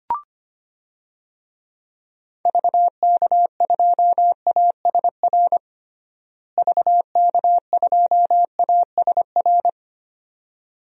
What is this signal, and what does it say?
Morse code, heard as a keyed tone, about 25 wpm, 700 Hz. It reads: VK2ASR VK2ASR
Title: DXCC-100 Call Signs - 6 Characters - Repeated 2x - 25wpm